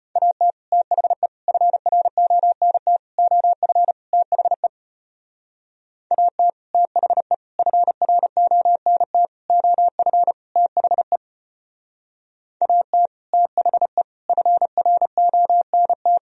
Morse code, heard as a keyed tone, about 32 wpm, 700 Hz. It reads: AT THE FRONT OF THE AT THE FRONT OF THE AT THE FRONT